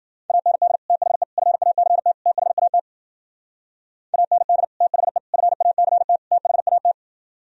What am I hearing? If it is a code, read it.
Morse code, heard as a keyed tone, about 51 wpm, 700 Hz. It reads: AND THE FACT THAT AND THE FACT THAT